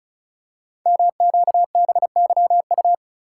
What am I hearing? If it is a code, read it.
Morse code, heard as a keyed tone, about 36 wpm, 700 Hz. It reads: MQBYU
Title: Code Groups - 5 Letters - 35wpm